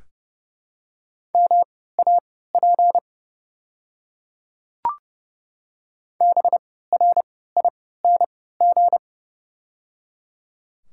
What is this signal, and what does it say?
Morse code, heard as a keyed tone, about 30 wpm, spaced as for 17 wpm, 700 Hz. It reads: MAP BRING
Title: 3x Character Spacing - Top 500 Words - 30wpm